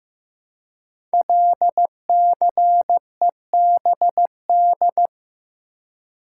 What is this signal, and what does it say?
Morse code, heard as a keyed tone, about 15 wpm, 700 Hz. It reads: LCEBD
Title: Code Groups - 5 Letters - 15wpm